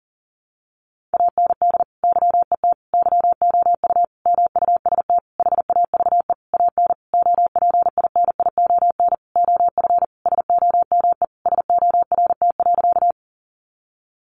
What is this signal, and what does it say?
Morse code, heard as a keyed tone, about 40 wpm, 700 Hz. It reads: AND YET YOU MUST HAVE AN OPINION OF SOME SORT.
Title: Sentences from Top 1300 Words 40wpm